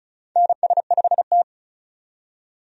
Morse code, heard as a keyed tone, about 34 wpm, 700 Hz. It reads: NS5T
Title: USA Call Signs - 4 Characters - 35wpm